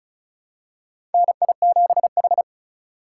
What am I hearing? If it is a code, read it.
Morse code, heard as a keyed tone, about 36 wpm, 700 Hz. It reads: NI7H
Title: USA Call Signs - 4 Characters - 4x Character Spacing - 35wpm